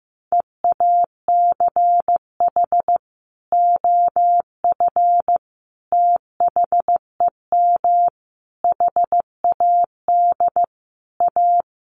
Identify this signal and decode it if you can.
Morse code, heard as a keyed tone, about 15 wpm, 700 Hz. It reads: EACH OF THEM HAD A
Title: Sentences from Top 1300 Words 15wpm